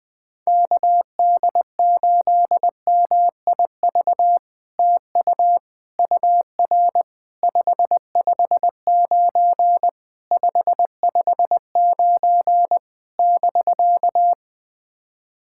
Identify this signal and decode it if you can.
Morse code, heard as a keyed tone, about 20 wpm, 700 Hz. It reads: KD8MIV TU UR 559 559 <BK>
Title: POTA QSO Practice - 20wpm